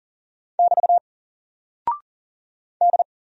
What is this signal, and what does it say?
Morse code, heard as a keyed tone, about 38 wpm, 700 Hz. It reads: <BT> D
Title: Single Character (Letter:Number:Punctuation:Prosign) 40wpm